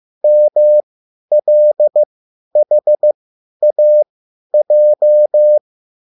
Morse code, heard as a keyed tone, about 15 wpm, spaced as for 10 wpm, 600 Hz. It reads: MLHAJ
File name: QSL / CW lesson